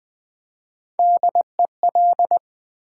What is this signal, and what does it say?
Morse code, heard as a keyed tone, about 20 wpm, 700 Hz. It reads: DEL
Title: Build A Word - 3 Letters At A Time - 8-Letter Words - 20wpm